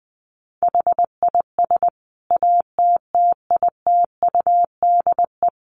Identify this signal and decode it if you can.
Morse code, heard as a keyed tone, about 20 wpm, 700 Hz. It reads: HIS ATTITUDE
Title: Sentences from Top 1300 Words 20wpm